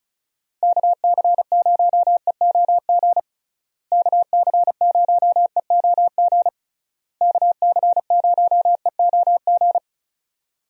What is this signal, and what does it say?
Morse code, heard as a keyed tone, about 34 wpm, 700 Hz. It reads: KC0EOG KC0EOG KC0EOG
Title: USA Call Signs - 6 Characters - Repeated 3x - 35wpm